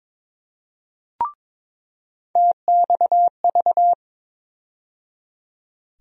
Morse code, heard as a keyed tone, about 22 wpm, 700 Hz. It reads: TXV
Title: USA Call Sign Postfixes - 3 Letters 22wpm